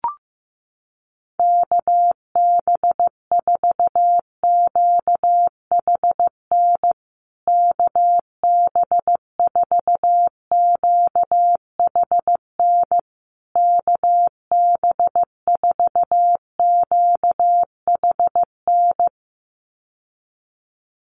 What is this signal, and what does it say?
Morse code, heard as a keyed tone, about 15 wpm, 700 Hz. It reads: KB4QHN KB4QHN KB4QHN
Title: USA Call Signs Repeated 3x 15wpm